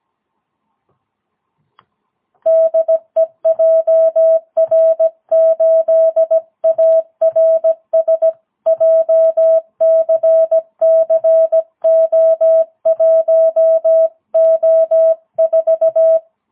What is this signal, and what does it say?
Morse code, heard as a keyed tone, about 17 wpm, 650 Hz. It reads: DE JR8ARS JCCO1O4
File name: jh8cbh / 2022cw